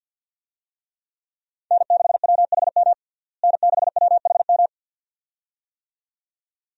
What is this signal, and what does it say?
Morse code, heard as a keyed tone, about 52 wpm, 700 Hz. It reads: N6WHM N6WHM
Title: USA Call Signs - 5 Characters - Repeated 2x - 50wpm